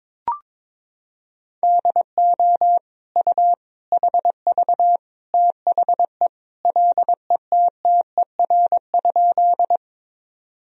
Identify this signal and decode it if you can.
Morse code, heard as a keyed tone, about 22 wpm, 700 Hz. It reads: DO U HV THE LETTER?
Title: Sentences from Top 300 Words and Abbreviations 22wpm